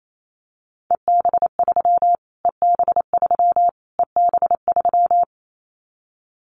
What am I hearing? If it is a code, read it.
Morse code, heard as a keyed tone, about 28 wpm, 700 Hz. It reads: EB3 EB3 EB3
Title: Callsign Prefixes for Top 25 DXCC - Repeated 3x - 28wpm